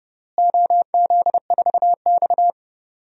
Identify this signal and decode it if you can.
Morse code, heard as a keyed tone, about 32 wpm, 700 Hz. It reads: OZ4X
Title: DXCC-100 Call Signs - 4 Characters - Repeated 3x - 30wpm